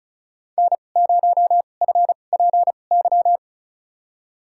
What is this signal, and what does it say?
Morse code, heard as a keyed tone, about 36 wpm, spaced as for 25 wpm, 700 Hz. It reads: N0FPY
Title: USA Call Signs - 5 Characters - 2x Character Spacing - 35wpm